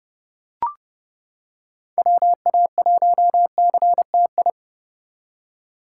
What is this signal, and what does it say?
Morse code, heard as a keyed tone, about 29 wpm, 700 Hz. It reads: WA1CTI